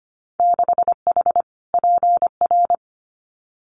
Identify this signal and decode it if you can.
Morse code, heard as a keyed tone, about 25 wpm, 700 Hz. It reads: 6H PR
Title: Winter Field Day Exchanges Encore 25wpm